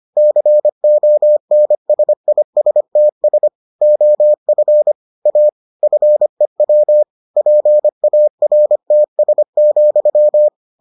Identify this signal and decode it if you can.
Morse code, heard as a keyed tone, about 25 wpm, 600 Hz. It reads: CONSISTS OF A FEW PARTS,